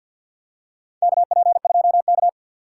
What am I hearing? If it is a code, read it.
Morse code, heard as a keyed tone, about 47 wpm, 700 Hz. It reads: KP2K
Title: USA Call Signs - 4x Character Spacing - 50wpm